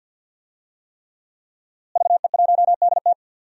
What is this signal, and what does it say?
Morse code, heard as a keyed tone, about 49 wpm, 700 Hz. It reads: VE1DT
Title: DXCC-100 Call Signs - 5 Characters - Repeated 3x - 50wpm